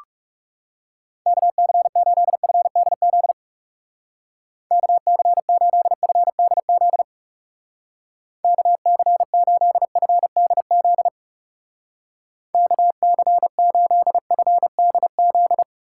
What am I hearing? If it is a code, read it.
Morse code, heard as a keyed tone, about 36 wpm, 700 Hz. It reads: KC8FDZ KC8FDZ KC8FDZ KC8FDZ
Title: Speed Racing - USA 2x3 Call Signs 30wpm